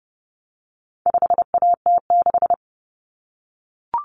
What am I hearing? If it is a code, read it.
Morse code, heard as a keyed tone, about 30 wpm, 700 Hz. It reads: 5AT6
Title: Common Cut Numbers - 4-Digit 30wpm